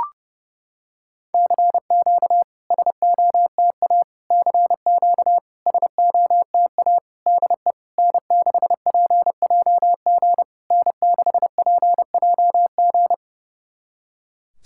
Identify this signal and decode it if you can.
Morse code, heard as a keyed tone, about 30 wpm, 690 Hz. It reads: CQ SOTA CQ SOTA DE N6PJG N6PJG
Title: SOTA QSO Practice 30wpm